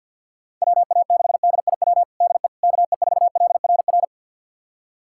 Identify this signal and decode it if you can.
Morse code, heard as a keyed tone, about 47 wpm, 700 Hz. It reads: WA6DIW DE KE4LRR
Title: USA Callsign DE Callsign 50wpm